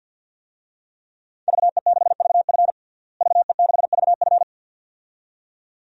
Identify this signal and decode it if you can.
Morse code, heard as a keyed tone, about 52 wpm, 700 Hz. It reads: VE6VF VE6VF
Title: DXCC-100 Call Signs - 5 Characters - Repeated 2x - 50wpm